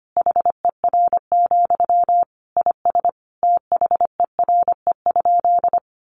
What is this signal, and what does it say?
Morse code, heard as a keyed tone, about 25 wpm, 700 Hz. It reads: HER, IS THERE?
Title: Sentences from Top 1500 Words 25wpm